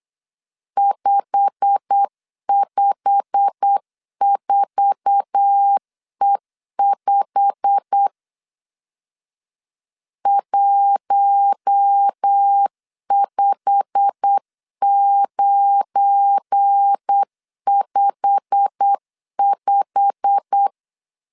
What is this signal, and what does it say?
Morse code, heard as a keyed tone, about 8 wpm, 800 Hz. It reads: 554E5 15955
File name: LEZIONI / 8 YJ45 / 8 YJ45 8wpm 3214